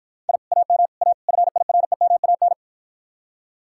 Morse code, heard as a keyed tone, about 52 wpm, 700 Hz. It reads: I AM A FIREMAN